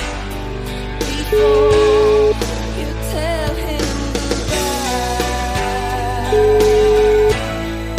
Ringback tone (Europe, 425 Hz), with noise.